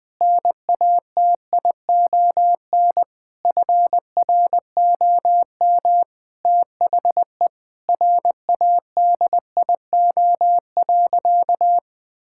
Morse code, heard as a keyed tone, about 20 wpm, 700 Hz. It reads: NATION FROM THE RADIO.
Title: Sentences with 5-Word Phrases - 20wpm